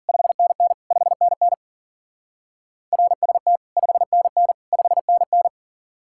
Morse code, heard as a keyed tone, about 51 wpm, 700 Hz. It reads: 5NN 5NN RST 5NN 5NN